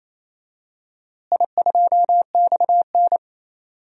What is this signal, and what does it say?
Morse code, heard as a keyed tone, about 28 wpm, 700 Hz. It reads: I2XN
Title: DXCC-100 Call Signs - 4 Characters - Repeated 2x - 28wpm